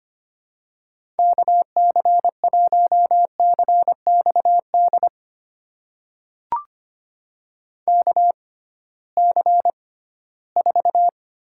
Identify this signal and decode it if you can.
Morse code, heard as a keyed tone, about 25 wpm, spaced as for 16 wpm, 700 Hz. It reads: KC1CXD KC4
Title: USA Call Signs - 6 Characters - 6x Character Spacing - 25wpm